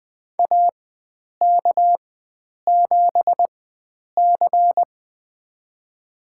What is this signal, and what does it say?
Morse code, heard as a keyed tone, about 20 wpm, spaced as for 9 wpm, 700 Hz. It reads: AK7C
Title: USA Call Signs - 4 Characters - 4x Character Spacing - 20wpm